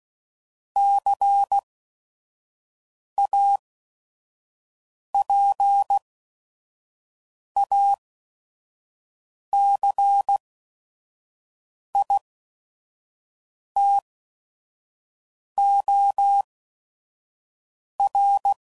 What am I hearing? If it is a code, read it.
Morse code, heard as a keyed tone, about 16 wpm, spaced as for 5 wpm, 800 Hz. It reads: CAPACITOR
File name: CW Code Groups @5WPM / Technical Subjects / XC